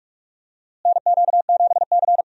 Morse code, heard as a keyed tone, about 46 wpm, 700 Hz. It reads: NQ7C
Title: USA Call Signs - 4 Characters - Repeated 3x - 45wpm